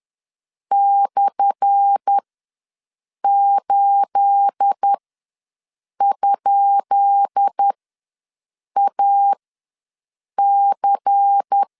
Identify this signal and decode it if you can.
Morse code, heard as a keyed tone, about 10 wpm, spaced as for 6 wpm, 800 Hz. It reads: /8?AC
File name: LEZIONI / 10 SIGNS / 10B INT ws 10wpm 4134